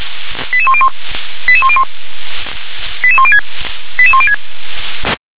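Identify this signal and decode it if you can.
Five-tone selective call (ZVEI), heard as quick runs of five tones, tones of about 70 ms. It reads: 88101 88101 88106 88106
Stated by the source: ZVEI five-tone call 88126 received on SDR.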